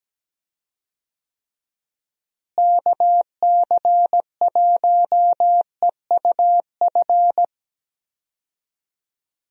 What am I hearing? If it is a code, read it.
Morse code, heard as a keyed tone, about 17 wpm, 700 Hz. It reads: KC1EUF
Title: USA Call Signs - 6 Characters - Repeated 2x - 17wpm